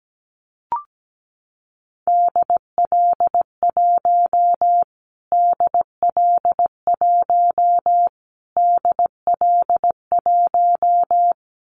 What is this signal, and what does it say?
Morse code, heard as a keyed tone, about 17 wpm, 700 Hz. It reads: DL1 DL1 DL1
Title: Callsign Prefixes for Top 25 DXCC - Repeated 3x - 17wpm